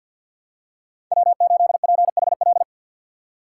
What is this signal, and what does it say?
Morse code, heard as a keyed tone, about 50 wpm, 700 Hz. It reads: W8PHL
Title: USA Call Signs - 5 Characters - 3x Character Spacing - 50wpm